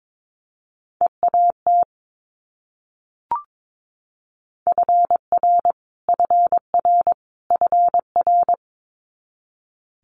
Morse code, heard as a keyed tone, about 22 wpm, 700 Hz. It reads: EAT FR FR FR
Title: Multi-letter Phonemes Repeated 3x 22wpm